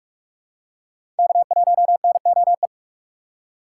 Morse code, heard as a keyed tone, about 44 wpm, 690 Hz. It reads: K1NOE
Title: USA Call Signs - 4x Character Spacing - 45wpm